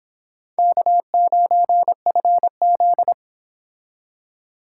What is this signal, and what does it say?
Morse code, heard as a keyed tone, about 26 wpm, 700 Hz. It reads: K9FZ